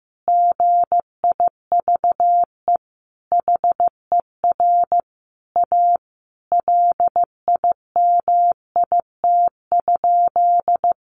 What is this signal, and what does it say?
Morse code, heard as a keyed tone, about 15 wpm, 700 Hz. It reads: GIVE HER A LIMIT?
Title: Sentences from Top 1500 Words 15wpm